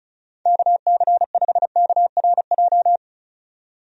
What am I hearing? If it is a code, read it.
Morse code, heard as a keyed tone, about 34 wpm, 700 Hz. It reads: KC5KRJ
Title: USA Call Signs - 6 Characters - 35wpm